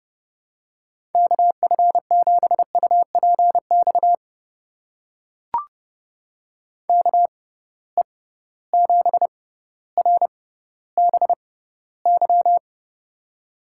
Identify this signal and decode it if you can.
Morse code, heard as a keyed tone, about 30 wpm, spaced as for 15 wpm, 700 Hz. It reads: KF7UPX KE7RBY
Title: USA Call Signs - 6 Characters - 6x Character Spacing - 30wpm